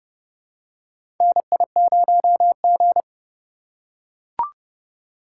Morse code, heard as a keyed tone, about 30 wpm, 700 Hz. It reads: NI0G
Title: USA Call Signs - 2x Character Spacing - 30wpm